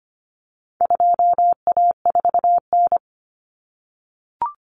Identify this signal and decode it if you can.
Morse code, heard as a keyed tone, about 25 wpm, 700 Hz. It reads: 2A4N
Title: Common Cut Numbers - 4-Digit 25wpm